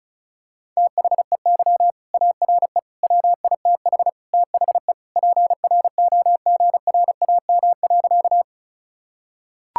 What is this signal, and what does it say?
Morse code, heard as a keyed tone, about 36 wpm, 700 Hz. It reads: THEY ARE WITH THE PROGRAM.
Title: Sentences from Top 1000 Words - Review Part 5 - 35wpm